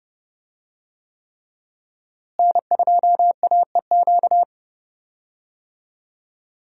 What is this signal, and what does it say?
Morse code, heard as a keyed tone, about 31 wpm, 700 Hz. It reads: N2AEQ
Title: USA Call Signs - 5 Characters - Repeated 3x - 30wpm